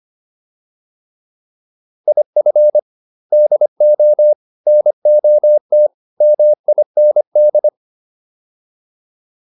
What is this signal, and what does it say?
Morse code, heard as a keyed tone, about 25 wpm, 600 Hz. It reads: IF DO NOT MIND